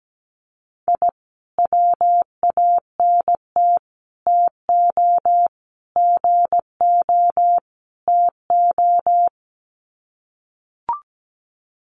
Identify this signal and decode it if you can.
Morse code, heard as a keyed tone, about 17 wpm, 700 Hz. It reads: I WANT TO GO TO
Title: Sets of 5 Words Encore 17wpm